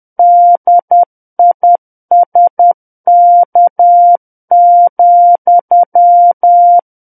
Morse code, heard as a keyed tone, about 10 wpm, 700 Hz. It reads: DISK,
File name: Book One / 01-Chapter 1 - 10 WPM